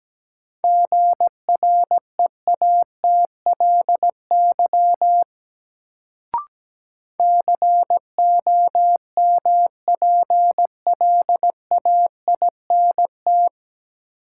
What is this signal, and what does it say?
Morse code, heard as a keyed tone, about 17 wpm, 700 Hz. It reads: GREATLY COMPLAINT
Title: Sentences from Top 1400 Words - New Word Drill Alt - 17wpm